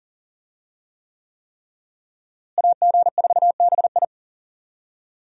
Morse code, heard as a keyed tone, about 40 wpm, 700 Hz. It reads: AG4BI
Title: USA Call Signs - 5 Characters - Repeated 3x - 40wpm